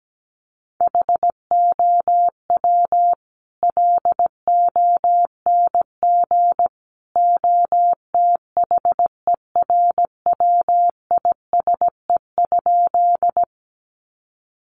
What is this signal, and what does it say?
Morse code, heard as a keyed tone, about 17 wpm, 700 Hz. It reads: HOW LONG OTHERWISE?
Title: Sentences from Top 1200 Words 17wpm